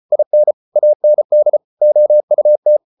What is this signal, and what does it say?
Morse code, heard as a keyed tone, about 34 wpm, 600 Hz. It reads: IN AND OUT